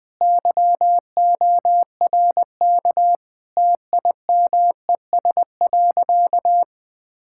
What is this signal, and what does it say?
Morse code, heard as a keyed tone, about 20 wpm, 700 Hz. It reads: YORK TIMES.
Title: Sentences with 5-Word Phrases - Spoken Beginning - 20wpm